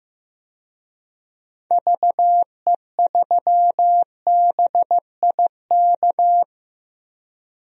Morse code, heard as a keyed tone, about 15 wpm, 700 Hz. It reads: VE3BIK
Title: DXCC-100 Call Signs - 6 Characters - Repeated 3x - 15wpm